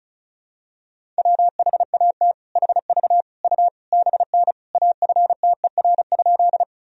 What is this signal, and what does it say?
Morse code, heard as a keyed tone, about 36 wpm, 700 Hz. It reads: WHAT HV U BN AFTER?
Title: Sentences from Top 200 Words and Abbreviations 35wpm